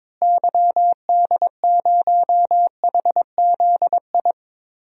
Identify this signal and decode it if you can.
Morse code, heard as a keyed tone, about 22 wpm, 700 Hz. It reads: YD0HZI
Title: DXCC-100 Call Signs - 6 Characters - Repeated 3x - 22wpm